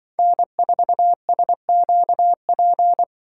Morse code, heard as a keyed tone, about 25 wpm, 700 Hz. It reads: N4SQP
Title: Speed Racing - USA 1x3 Call Signs 20wpm